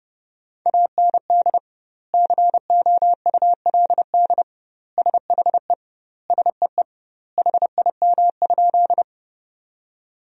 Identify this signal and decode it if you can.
Morse code, heard as a keyed tone, about 32 wpm, 700 Hz. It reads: AND COULD SHE SEE HIM?
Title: 2x Word-Spacing - Sentences from Top 100 Words - 30wpm